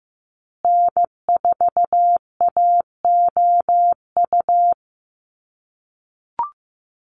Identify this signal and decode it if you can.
Morse code, heard as a keyed tone, about 15 wpm, 700 Hz. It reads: N4AOU
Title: USA Call Signs Repeated 2x 15wpm